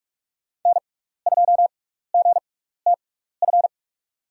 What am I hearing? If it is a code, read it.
Morse code, heard as a keyed tone, about 47 wpm, spaced as for 16 wpm, 700 Hz. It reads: N2GTF
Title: USA Call Signs - 5 Characters - 6x Character Spacing - 45wpm